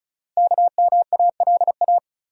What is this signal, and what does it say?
Morse code, heard as a keyed tone, about 36 wpm, 690 Hz. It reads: KMALA
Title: Code Groups - 5 Letters - 35wpm